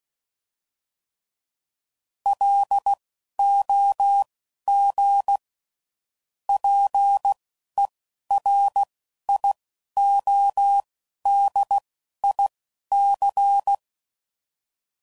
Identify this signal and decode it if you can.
Morse code, heard as a keyed tone, about 16 wpm, spaced as for 11 wpm, 800 Hz. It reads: LOG PERIODIC
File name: CW Code Groups @10WPM / fromSM's / Antennas